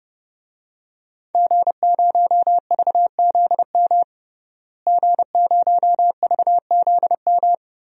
Morse code, heard as a keyed tone, about 30 wpm, 700 Hz. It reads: G0VZM G0VZM
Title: DXCC-100 Call Signs - 5 Characters - Repeated 3x - 30wpm